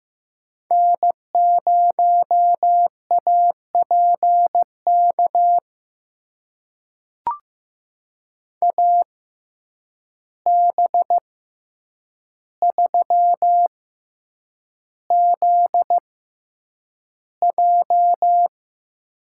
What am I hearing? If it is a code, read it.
Morse code, heard as a keyed tone, about 15 wpm, spaced as for 8 wpm, 700 Hz. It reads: N0APK AB3ZJ
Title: USA Call Signs - 5 Characters - 6x Character Spacing - 15wpm